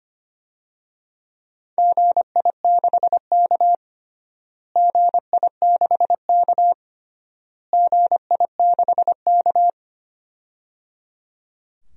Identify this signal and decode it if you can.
Morse code, heard as a keyed tone, about 25 wpm, 700 Hz. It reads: GI6K GI6K GI6K
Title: DXCC-100 Call Signs - 4 Characters - Repeated 3x - 25wpm